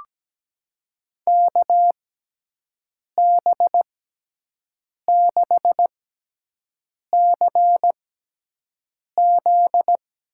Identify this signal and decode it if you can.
Morse code, heard as a keyed tone, about 17 wpm, spaced as for 6 wpm, 700 Hz. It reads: KB6CZ
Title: USA Call Signs - 6 Characters - 6x Character Spacing - 17wpm